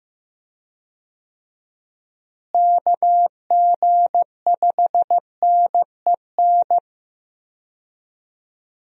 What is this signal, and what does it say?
Morse code, heard as a keyed tone, about 15 wpm, 700 Hz. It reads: KG5NEN
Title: USA Call Signs - 6 Characters - Repeated 2x - 15wpm